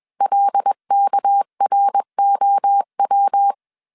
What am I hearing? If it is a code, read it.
Morse code, heard as a keyed tone, about 21 wpm, 800 Hz. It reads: LKROW
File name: CW / DET / 1 6 1500C S B 20wpm 2048